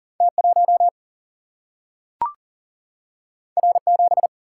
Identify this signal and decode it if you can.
Morse code, heard as a keyed tone, about 39 wpm, 700 Hz. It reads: T 1 R 7
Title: Single-Letter Single-Number 40wpm